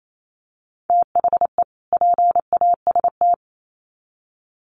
Morse code, heard as a keyed tone, about 28 wpm, 700 Hz. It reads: THE PAST
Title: Sets of 2 Words 28wpm